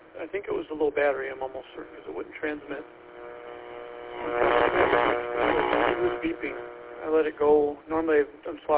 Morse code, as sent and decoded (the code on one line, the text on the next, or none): none